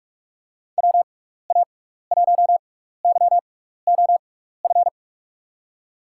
Morse code, heard as a keyed tone, about 41 wpm, spaced as for 15 wpm, 700 Hz. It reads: WA1YOF
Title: USA Call Signs - 6x Character Spacing - 45wpm